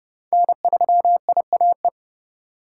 Morse code, heard as a keyed tone, about 31 wpm, 700 Hz. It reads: N3IAE
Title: USA Call Signs - 5 Characters - 4x Character Spacing - 30wpm